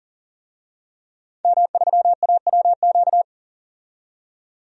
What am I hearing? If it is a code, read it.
Morse code, heard as a keyed tone, about 40 wpm, 700 Hz. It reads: M3AWQ
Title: DXCC-100 Call Signs - 5 Characters - 40wpm